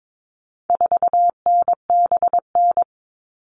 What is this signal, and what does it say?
Morse code, heard as a keyed tone, about 22 wpm, 700 Hz. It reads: 4NBN
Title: All Cut Numbers - 4-Digit 22wpm